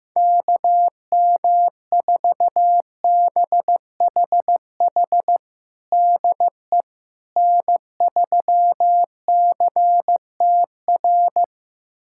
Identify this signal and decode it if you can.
Morse code, heard as a keyed tone, about 16 wpm, 700 Hz. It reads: KM4BHH DE N3CTR
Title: USA Callsign DE Callsign 15wpm